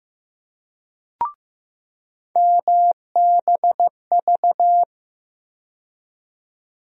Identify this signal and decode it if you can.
Morse code, heard as a keyed tone, about 15 wpm, 700 Hz. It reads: MBV
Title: USA Call Sign Postfixes - 3 Letters 15wpm